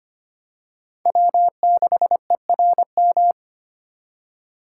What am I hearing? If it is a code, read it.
Morse code, heard as a keyed tone, about 25 wpm, 700 Hz. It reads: W6ERM
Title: USA Call Signs - 5 Characters - 2x Character Spacing - 25wpm